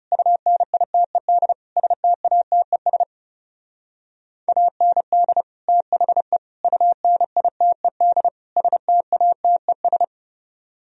Morse code, heard as a keyed tone, about 39 wpm, 700 Hz. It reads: UNITED STATES AND THE UNITED STATES